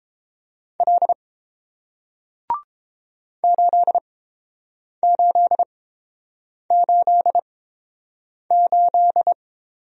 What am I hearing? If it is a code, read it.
Morse code, heard as a keyed tone, about 31 wpm, spaced as for 7 wpm, 700 Hz. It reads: L 8888